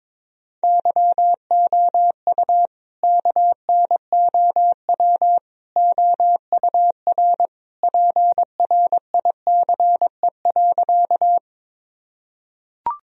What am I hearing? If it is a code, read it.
Morse code, heard as a keyed tone, about 22 wpm, 700 Hz. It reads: YOU KNOW OUR PRICE.
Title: Sentences from Top 1000 Words - Review Part 4 - 22wpm